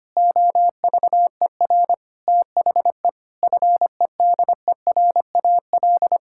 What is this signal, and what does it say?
Morse code, heard as a keyed tone, about 25 wpm, 700 Hz. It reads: OVER THE FEDERAL
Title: Sentences from Top 1000 Words - Review Part 5 - 25wpm